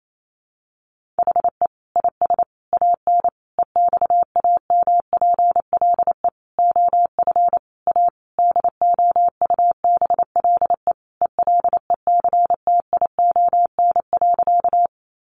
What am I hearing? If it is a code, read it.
Morse code, heard as a keyed tone, about 28 wpm, 700 Hz. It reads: HE IS AN EXAMPLE OF A DOUBLE ELECTION.
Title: Sentences from Top 1500 Words 28wpm